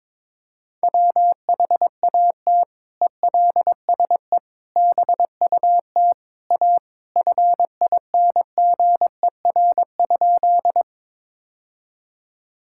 Morse code, heard as a keyed tone, about 22 wpm, 700 Hz. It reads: WHAT ELSE BUT A FINGER?
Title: Sentences from Top 1600 Words 22wpm